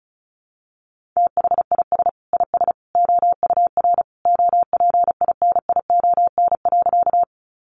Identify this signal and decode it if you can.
Morse code, heard as a keyed tone, about 34 wpm, 700 Hz. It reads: THIS IS OUR OPINION.
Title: Sentences from Top 1200 Words 35wpm